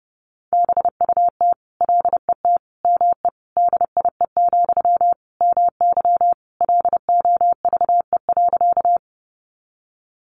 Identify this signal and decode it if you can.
Morse code, heard as a keyed tone, about 30 wpm, 700 Hz. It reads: BUT LET ME DIE, MY LOVE.
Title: Sentences from Top 1300 Words 30wpm